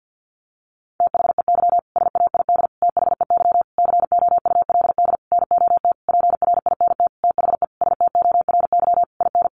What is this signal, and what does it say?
Morse code, heard as a keyed tone, about 50 wpm, 700 Hz. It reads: THEY SAID THEY COULD NOT PRINT THE STORY IN
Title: Sentences from Top 1400 Words 50wpm